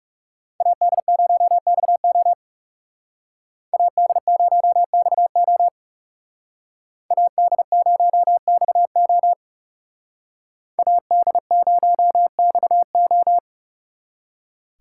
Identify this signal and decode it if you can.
Morse code, heard as a keyed tone, about 36 wpm, 700 Hz. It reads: AD0XO AD0XO AD0XO AD0XO
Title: Speed Racing - USA 2x2 Call Signs 30wpm